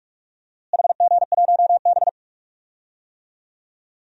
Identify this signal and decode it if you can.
Morse code, heard as a keyed tone, about 47 wpm, 700 Hz. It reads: HG1B
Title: DXCC-100 Call Signs - 4 Characters - Repeated 3x - 45wpm